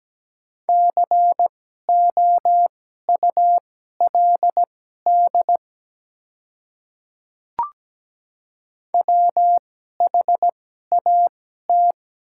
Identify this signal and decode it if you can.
Morse code, heard as a keyed tone, about 17 wpm, spaced as for 12 wpm, 700 Hz. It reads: COULD WHAT
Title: 2x Character Spacing - Top 100 Words - 17wpm